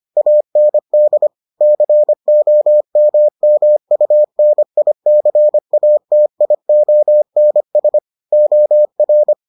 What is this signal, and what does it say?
Morse code, heard as a keyed tone, about 25 wpm, 600 Hz. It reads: AND COMMUNICATIONS OR